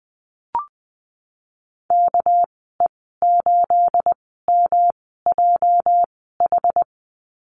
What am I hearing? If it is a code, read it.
Morse code, heard as a keyed tone, about 20 wpm, spaced as for 15 wpm, 700 Hz. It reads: KE8MJH
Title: 219 Lesson 35g - Number 0 - Call Signs 20wpm